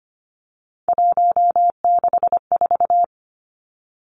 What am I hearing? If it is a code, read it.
Morse code, heard as a keyed tone, about 25 wpm, 700 Hz. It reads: 164
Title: Numbers - 3-Digit 25wpm